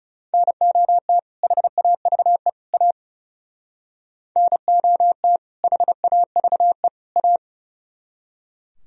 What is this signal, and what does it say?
Morse code, heard as a keyed tone, about 38 wpm, 700 Hz. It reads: NOT HAVE A NOT HAVE A